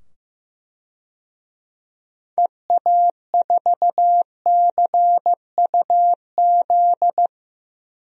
Morse code, heard as a keyed tone, about 15 wpm, 700 Hz. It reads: EA4CUZ
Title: DXCC-100 Call Signs - 6 Characters - 15wpm